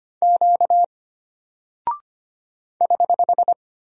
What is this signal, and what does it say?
Morse code, heard as a keyed tone, about 25 wpm, 700 Hz. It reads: Q <HH>
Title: Single Character (Letter:Number:Punctuation:Prosign) Alt 25wpm